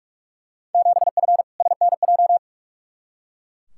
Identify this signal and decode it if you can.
Morse code, heard as a keyed tone, about 41 wpm, 700 Hz. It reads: 7F SNJ
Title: Field Day Exchanges Encore - 45wpm